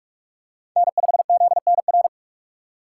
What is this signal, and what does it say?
Morse code, heard as a keyed tone, about 42 wpm, 700 Hz. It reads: N5ZNR
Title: USA Call Signs - 5 Characters - 3x Character Spacing - 45wpm